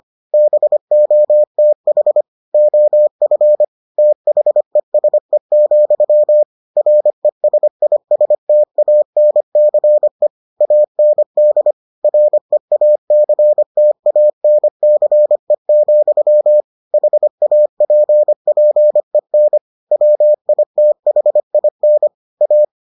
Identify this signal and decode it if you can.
Morse code, heard as a keyed tone, about 25 wpm, 600 Hz. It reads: BOTH OF THESE, RESISTANCE AND REACTANCE, HAPPEN WITHIN A